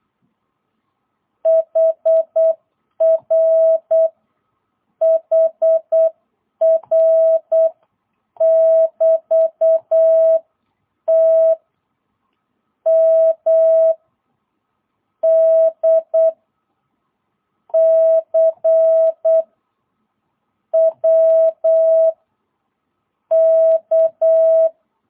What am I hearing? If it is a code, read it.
Morse code, heard as a keyed tone, about 8 wpm, 660 Hz. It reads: HR HR <BT> T M D C W K
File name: jh8cbh / 2022cw